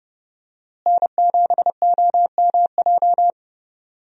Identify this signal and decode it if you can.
Morse code, heard as a keyed tone, about 30 wpm, 700 Hz. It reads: N7OMJ
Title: USA Call Signs - 5 Characters - Repeated 2x - 30wpm